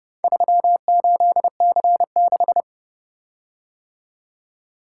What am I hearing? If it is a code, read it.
Morse code, heard as a keyed tone, about 29 wpm, 700 Hz. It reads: 38C6